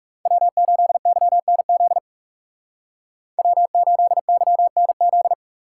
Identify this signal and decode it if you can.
Morse code, heard as a keyed tone, about 37 wpm, 700 Hz. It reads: W8YNZ W8YNZ
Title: Speed Racing - USA Call Signs 30wpm